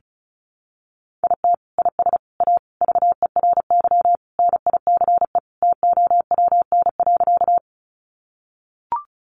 Morse code, heard as a keyed tone, about 34 wpm, 700 Hz. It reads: IT IS A VERY NICE TOWN.
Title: Sentences from Top 1100 Words 35wpm